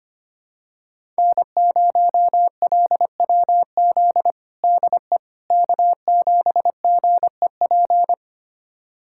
Morse code, heard as a keyed tone, about 25 wpm, 700 Hz. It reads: N0LWZ DE K7GEP
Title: USA Callsign DE Callsign - 5 Characters - 25wpm